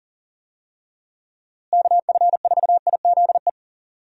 Morse code, heard as a keyed tone, about 40 wpm, 700 Hz. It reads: KF4IZE
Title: USA Call Signs - 6 Characters - Repeated 2x - 40wpm